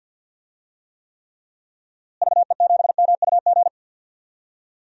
Morse code, heard as a keyed tone, about 49 wpm, 700 Hz. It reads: VE7MUG
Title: DXCC-100 Call Signs - 6 Characters - Repeated 3x - 50wpm